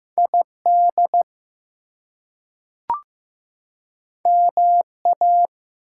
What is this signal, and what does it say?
Morse code, heard as a keyed tone, about 15 wpm, 700 Hz. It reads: ID MA
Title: US States and Canadian Provinces 15wpm